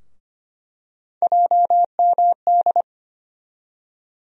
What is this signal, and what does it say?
Morse code, heard as a keyed tone, about 25 wpm, 700 Hz. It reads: JMD
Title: USA Call Sign Postfixes - 3 Letters 25wpm